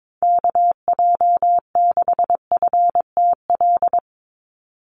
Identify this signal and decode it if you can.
Morse code, heard as a keyed tone, about 22 wpm, 700 Hz. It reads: KJ6FTL
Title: USA Call Signs 22wpm